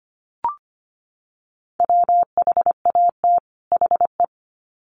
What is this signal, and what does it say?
Morse code, heard as a keyed tone, about 25 wpm, 700 Hz. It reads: WHAT HE
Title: Sets of 2 Words 25wpm